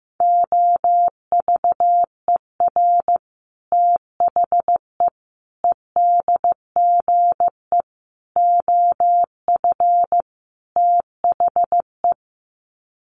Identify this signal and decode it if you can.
Morse code, heard as a keyed tone, about 15 wpm, 700 Hz. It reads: OVER THE EDGE OF THE
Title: Sets of 5 Words Encore2 15wpm